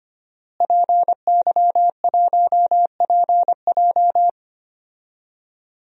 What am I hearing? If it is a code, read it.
Morse code, heard as a keyed tone, about 25 wpm, 700 Hz. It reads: PY1PJ
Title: DXCC-100 Call Signs - 5 Characters - Repeated 3x - 25wpm